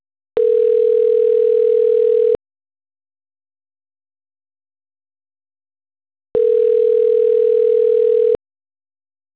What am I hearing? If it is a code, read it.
Ringback tone (North America), 440 + 480 Hz.
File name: autotestagent / 10-35-26.136..callcentertransfer. .1b231b36-c3c3-4fdc-a98d